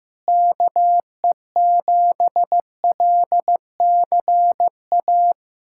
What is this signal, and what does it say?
Morse code, heard as a keyed tone, about 15 wpm, 700 Hz. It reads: KE7LCA
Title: USA Callsign DE Callsign - 6 Characters - 15wpm